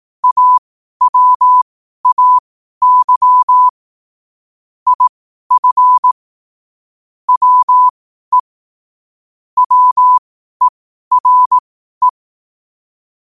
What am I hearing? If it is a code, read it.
Morse code, heard as a keyed tone, about 18 wpm, spaced as for 12 wpm, 1000 Hz. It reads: AWAY IF WE WERE
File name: 12wpm